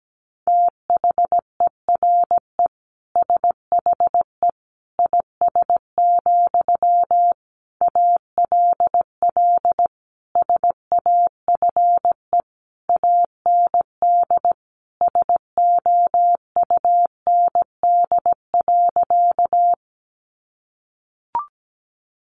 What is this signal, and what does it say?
Morse code, heard as a keyed tone, about 17 wpm, 700 Hz. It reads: THERE SHE IS, ALL SAFE AND SOUND.
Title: Sentences from Top 900 Words 17wpm